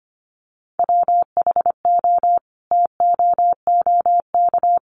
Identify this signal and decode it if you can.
Morse code, heard as a keyed tone, about 25 wpm, 700 Hz. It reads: WHO TOOK
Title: Sentences from Top 300 Words 25wpm